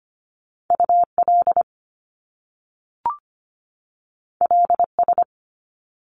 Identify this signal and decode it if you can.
Morse code, heard as a keyed tone, about 25 wpm, 700 Hz. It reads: UL LS
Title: Top Two Letter Combinations in Words 25wpm